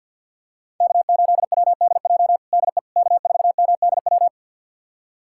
Morse code, heard as a keyed tone, about 48 wpm, 700 Hz. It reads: K8WDJ DE K4MDW
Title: USA Callsign DE Callsign - 5 Characters - 50wpm